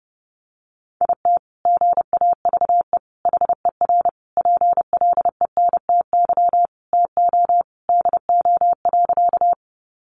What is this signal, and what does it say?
Morse code, heard as a keyed tone, about 30 wpm, 700 Hz. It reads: IT GAVE HER PLENTY TO DO.
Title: Sentences from Top 1300 Words 30wpm